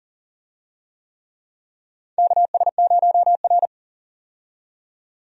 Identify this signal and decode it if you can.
Morse code, heard as a keyed tone, about 37 wpm, 700 Hz. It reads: KS0R
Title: USA Call Signs - 4 Characters - 40wpm